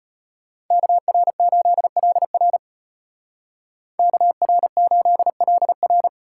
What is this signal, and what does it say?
Morse code, heard as a keyed tone, about 30 wpm, 700 Hz. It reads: KR8LR KR8LR
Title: Speed Racing - USA 2x2 Call Signs 25wpm